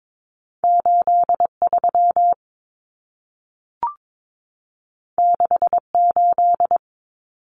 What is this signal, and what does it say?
Morse code, heard as a keyed tone, about 22 wpm, 700 Hz. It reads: 83 68
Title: Numbers - 2-Digit 22wpm